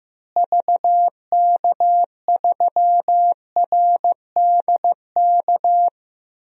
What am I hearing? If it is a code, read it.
Morse code, heard as a keyed tone, about 14 wpm, 700 Hz. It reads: VK3RDK
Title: DXCC-100 Call Signs - 6 Characters - Repeated 2x - 15wpm